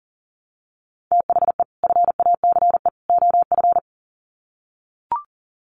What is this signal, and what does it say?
Morse code, heard as a keyed tone, about 40 wpm, 700 Hz. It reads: THE FACE OF
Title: Sets of 3 Words 40wpm